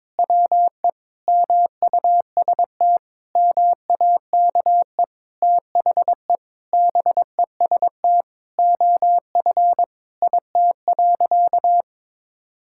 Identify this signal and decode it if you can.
Morse code, heard as a keyed tone, about 22 wpm, 700 Hz. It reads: WE MUST MAKE THE BEST OF IT.
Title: Sentences from Top 1000 Words - Review Part 5 - 22wpm